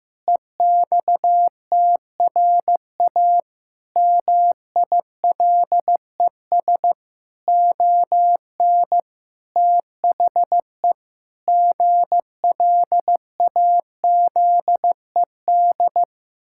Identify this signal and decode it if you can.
Morse code, heard as a keyed tone, about 16 wpm, 700 Hz. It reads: EXTRA MILES ON THE GLAZED
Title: Pangram Challenge v2 - 15wpm